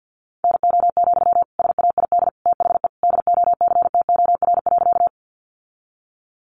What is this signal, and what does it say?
Morse code, heard as a keyed tone, about 48 wpm, 700 Hz. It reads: NO, SAID THE DOCTOR.
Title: Sentences from Top 1500 Words 50wpm